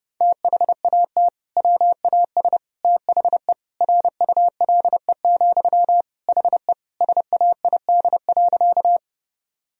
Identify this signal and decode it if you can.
Morse code, heard as a keyed tone, about 30 wpm, 700 Hz. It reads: THAT WAS THE RULE, HE SAID.
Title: Sentences from Top 1000 Words - Review Part 2 - 30wpm